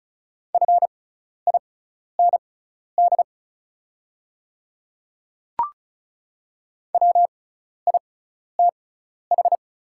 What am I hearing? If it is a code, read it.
Morse code, heard as a keyed tone, about 36 wpm, spaced as for 12 wpm, 700 Hz. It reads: FIND WITH
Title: 6x Character Spacing - Top 100 Words - 35wpm